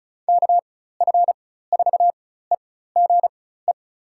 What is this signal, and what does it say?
Morse code, heard as a keyed tone, about 36 wpm, spaced as for 16 wpm, 700 Hz. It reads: KF4EGE
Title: USA Call Signs - 6 Characters - 4x Character Spacing - 35wpm